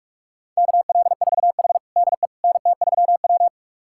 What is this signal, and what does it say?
Morse code, heard as a keyed tone, about 42 wpm, 700 Hz. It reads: KL4H DE NT3W
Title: USA Callsign DE Callsign - 4 Characters - 45wpm